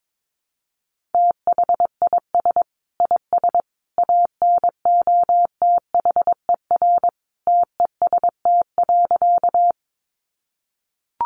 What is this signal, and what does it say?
Morse code, heard as a keyed tone, about 22 wpm, 700 Hz. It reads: THIS IS ANOTHER TEST.
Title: Sentences from Top 600 Words 22wpm